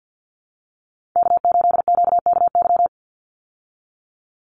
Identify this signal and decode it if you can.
Morse code, heard as a keyed tone, about 50 wpm, 700 Hz. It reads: K8QKY
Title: Call Signs Repeated 2x - 50wpm